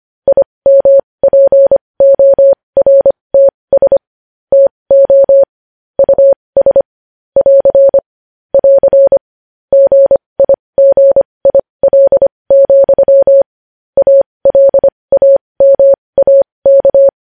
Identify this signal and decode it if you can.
Morse code, heard as a keyed tone, about 25 wpm, spaced as for 20 wpm, 550 Hz. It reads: IMPORTS TO US <AR> <AR> GIGIL, ALAMAK
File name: podcast / News headlines-1x-25x20